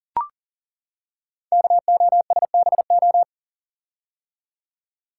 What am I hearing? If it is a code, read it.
Morse code, heard as a keyed tone, about 42 wpm, 690 Hz. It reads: KOSBO
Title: Code Groups - 5 Letters - 40wpm